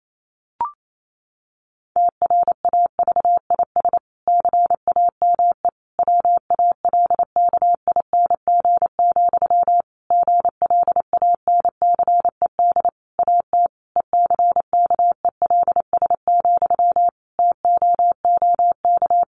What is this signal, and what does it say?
Morse code, heard as a keyed tone, about 28 wpm, 700 Hz. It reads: TRAVIS CAME WALKING, GLANCED AT ECKELS, TOOK
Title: A Sound Of Thunder 28wpm